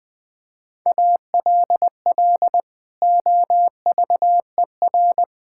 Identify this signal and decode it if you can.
Morse code, heard as a keyed tone, about 20 wpm, 700 Hz. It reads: ALL OVER
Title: Build A Phrase - 4-Word Phrases - 20wpm